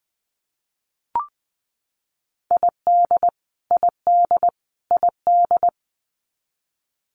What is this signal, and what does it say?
Morse code, heard as a keyed tone, about 20 wpm, 700 Hz. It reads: ID ID ID
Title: US States Repeated 3x 20wpm